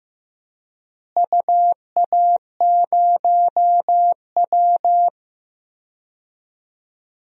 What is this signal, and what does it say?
Morse code, heard as a keyed tone, about 15 wpm, 700 Hz. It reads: UA0W
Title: DXCC-100 Call Signs - 4 Characters - Repeated 3x - 15wpm